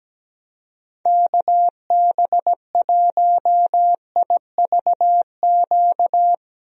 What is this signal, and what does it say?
Morse code, heard as a keyed tone, about 17 wpm, 700 Hz. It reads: KB1IVQ
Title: USA Call Signs - 6 Characters - Repeated 3x - 17wpm